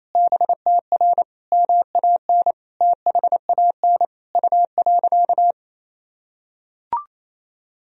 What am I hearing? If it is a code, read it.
Morse code, heard as a keyed tone, about 28 wpm, 700 Hz. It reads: BTR MAN THAN U.
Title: Sentences from Top 400 Words and Abbreviations 28wpm